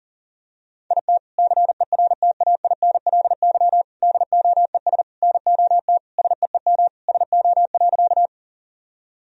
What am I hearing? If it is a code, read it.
Morse code, heard as a keyed tone, about 40 wpm, 700 Hz. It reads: IT CERTAINLY DOES NOT SEEM SO.
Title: Sentences from Top 1000 Words - Review Part 1 - 40wpm